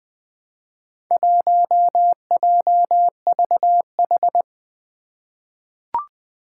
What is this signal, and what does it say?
Morse code, heard as a keyed tone, about 20 wpm, 700 Hz. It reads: 1JVH
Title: Four Challenging Letter-Number Combinations 20wpm